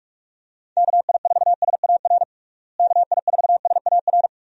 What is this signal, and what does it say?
Morse code, heard as a keyed tone, about 41 wpm, 700 Hz. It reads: KI4SAR KI4SAR
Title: USA Call Signs - 6 Characters - Repeated 3x - 45wpm